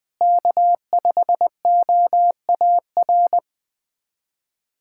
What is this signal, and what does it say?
Morse code, heard as a keyed tone, about 20 wpm, 700 Hz. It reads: K5OAR
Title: USA Callsign DE Callsign - 5 Characters - 20wpm